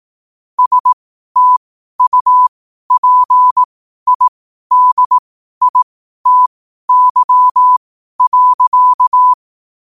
Morse code, heard as a keyed tone, about 18 wpm, spaced as for 12 wpm, 1000 Hz. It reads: STUPIDITY.
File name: qotdfiles / 12wpm